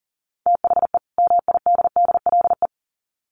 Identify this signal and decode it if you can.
Morse code, heard as a keyed tone, about 40 wpm, 700 Hz. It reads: THE MIDDLE
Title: Sets of 2 Words 40wpm